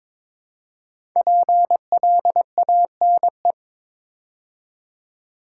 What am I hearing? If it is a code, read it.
Morse code, heard as a keyed tone, about 22 wpm, 700 Hz. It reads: PLANE
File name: 5 Letter Words - First 3 Letters Given - 22wpm